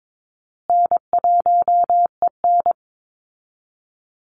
Morse code, heard as a keyed tone, about 22 wpm, 700 Hz. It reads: N1EN
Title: Call Signs Repeated 2x - 22wpm